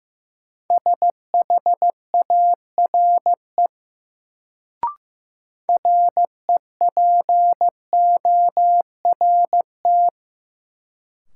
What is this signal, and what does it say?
Morse code, heard as a keyed tone, about 15 wpm, 700 Hz. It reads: SHARE REPORT
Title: Sentences from Top 900 Words - New Word Drill - 15wpm